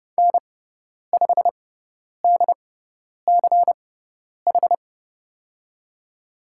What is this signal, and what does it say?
Morse code, heard as a keyed tone, about 31 wpm, spaced as for 10 wpm, 700 Hz. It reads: N5DCH
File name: Lesson 17g - Letter u - Call Signs 30wpm